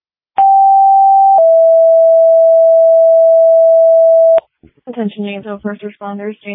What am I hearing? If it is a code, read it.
Two-tone page, heard as a one-second tone, then a three-second tone, about 780 then 660 Hz, with speech. It is heard from 0.4 to 4.4 s.